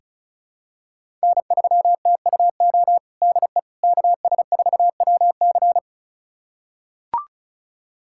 Morse code, heard as a keyed tone, about 36 wpm, 700 Hz. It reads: N3TUO DE KS4WC
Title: USA Callsign DE Callsign - 5 Characters - 35wpm